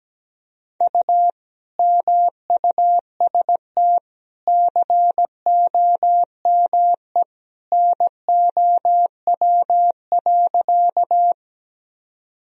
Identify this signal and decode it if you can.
Morse code, heard as a keyed tone, about 17 wpm, 700 Hz. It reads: U MUST COME NOW.
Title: Sentences from Top 200 Words and Abbreviations 17wpm